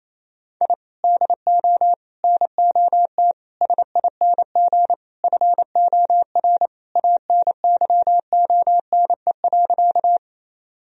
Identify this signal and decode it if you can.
Morse code, heard as a keyed tone, about 28 wpm, 700 Hz. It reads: I DO NOT SING FOR ANYONE.
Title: Sentences from Top 1000 Words - Review Part 2 - 28wpm